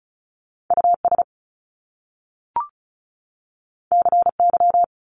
Morse code, heard as a keyed tone, about 36 wpm, 700 Hz. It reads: US CY
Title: Top Two Letter Combinations in Words 35wpm